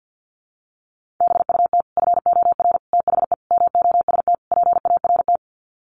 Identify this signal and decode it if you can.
Morse code, heard as a keyed tone, about 50 wpm, 700 Hz. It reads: BUT FOR THE MOST PART
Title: Sets of 5 Words Encore 50wpm